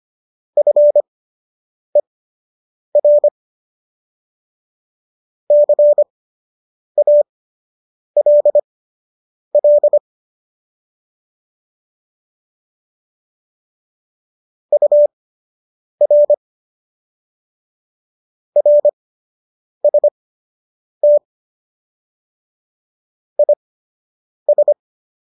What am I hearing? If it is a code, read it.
Morse code, heard as a keyed tone, about 25 wpm, spaced as for 8 wpm, 590 Hz. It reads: FER CALL UR RST IS